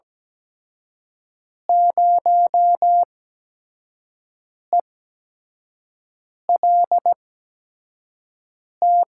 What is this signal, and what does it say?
Morse code, heard as a keyed tone, about 17 wpm, spaced as for 5 wpm, 700 Hz. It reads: 0ELT
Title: USA Call Signs - 5 Characters - 8x Character Spacing - 17wpm